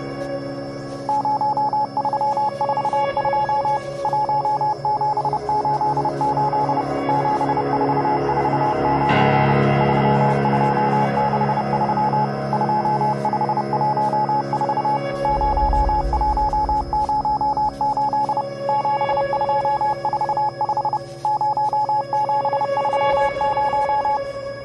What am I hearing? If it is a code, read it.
Morse code, heard as a keyed tone, about 29 wpm, 850 Hz, with noise. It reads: 0342 1788 7449 3263 2514 0108 7245 0730
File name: Ice - Telegraph 1344 7609 2575 (mp3cut.net)